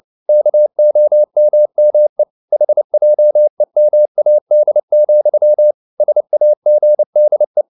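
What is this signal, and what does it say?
Morse code, heard as a keyed tone, about 29 wpm, 600 Hz. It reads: KOMME HJEMAD, SAGDE
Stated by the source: Fast (QRQ) Morse at 29 wpm, from CW med Gnister.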